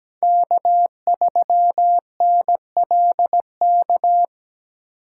Morse code, heard as a keyed tone, about 17 wpm, 700 Hz. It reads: K3NLK
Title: USA Call Signs - 3x Character Spacing - 17wpm